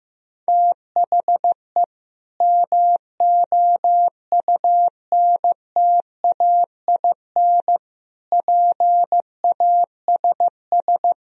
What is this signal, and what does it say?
Morse code, heard as a keyed tone, about 15 wpm, 700 Hz. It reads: THE MOUNTAIN PASS
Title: Sentences from Top 1000 Words - Review Part 1 - 15wpm